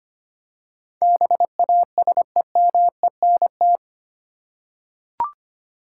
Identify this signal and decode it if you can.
Morse code, heard as a keyed tone, about 25 wpm, 700 Hz. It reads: BASEMENT
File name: Eight Letter Words - 25wpm